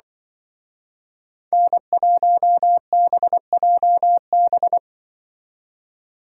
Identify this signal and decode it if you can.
Morse code, heard as a keyed tone, about 23 wpm, 700 Hz. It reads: N1BJB
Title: Speed Racing - USA 1x3 Call Signs 20wpm